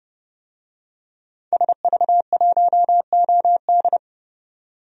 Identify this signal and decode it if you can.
Morse code, heard as a keyed tone, about 32 wpm, 700 Hz. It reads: SV1OD
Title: DXCC-100 Call Signs - 5 Characters - 30wpm